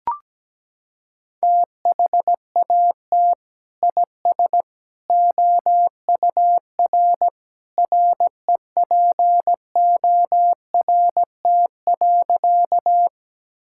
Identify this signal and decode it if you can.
Morse code, heard as a keyed tone, about 17 wpm, 700 Hz. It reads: THAT IS OUR REPORT.
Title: Sentences from Top 1000 Words - Review Part 6 - 17wpm